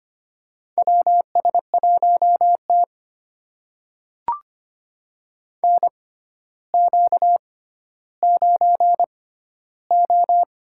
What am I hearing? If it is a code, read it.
Morse code, heard as a keyed tone, about 25 wpm, spaced as for 13 wpm, 700 Hz. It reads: WS1T NQ9O
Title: USA Call Signs - 4 Characters - 6x Character Spacing - 25wpm